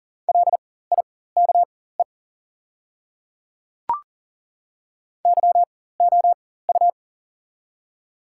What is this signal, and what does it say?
Morse code, heard as a keyed tone, about 40 wpm, spaced as for 19 wpm, 700 Hz. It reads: LIKE YOU
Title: 4x Character Spacing - Top 100 Words - 40wpm